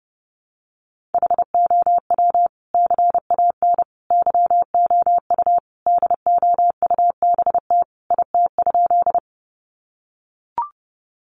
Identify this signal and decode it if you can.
Morse code, heard as a keyed tone, about 30 wpm, 700 Hz. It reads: HOW CAN YOU DOUBT IT?
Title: Sentences from Top 1500 Words 30wpm